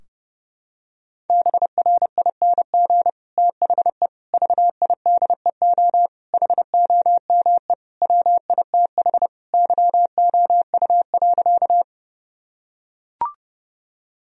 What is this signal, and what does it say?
Morse code, heard as a keyed tone, about 30 wpm, 700 Hz. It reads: BRING THE VIDEO HOME WITH YOU.
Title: Sentences from Top 1000 Words - Review Part 4 - 30wpm